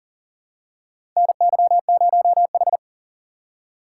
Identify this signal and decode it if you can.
Morse code, heard as a keyed tone, about 40 wpm, 700 Hz. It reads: NY0H
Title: USA Call Signs - 4 Characters - 4x Character Spacing - 40wpm